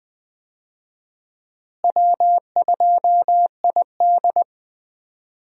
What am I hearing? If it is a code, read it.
Morse code, heard as a keyed tone, about 21 wpm, 700 Hz. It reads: W2ID
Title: USA Call Signs - 4 Characters - Repeated 3x - 20wpm